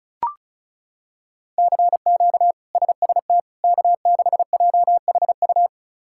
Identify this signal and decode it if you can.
Morse code, heard as a keyed tone, about 36 wpm, 700 Hz. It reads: CQ SST K6JHU
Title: K1USN SST QSO Practice - 35wpm